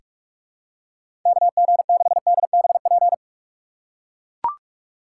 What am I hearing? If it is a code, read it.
Morse code, heard as a keyed tone, about 44 wpm, 700 Hz. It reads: KG6DBP
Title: USA Call Signs - 6 Characters - 4x Character Spacing - 45wpm